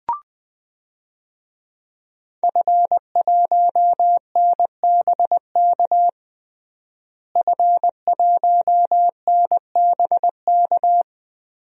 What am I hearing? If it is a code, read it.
Morse code, heard as a keyed tone, about 20 wpm, 700 Hz. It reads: F1NBK F1NBK
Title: DXCC-100 Call Signs - 5 Characters - Repeated 2x - 20wpm